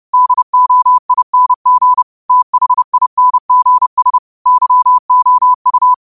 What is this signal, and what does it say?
Morse code, heard as a keyed tone, about 30 wpm, 1000 Hz. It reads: DOING THINGS YOU
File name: qotdfiles / 30wpm